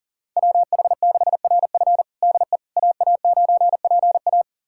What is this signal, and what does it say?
Morse code, heard as a keyed tone, about 37 wpm, 700 Hz. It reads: WH6RF DE AA9PA
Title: USA Callsign DE Callsign - 5 Characters - 40wpm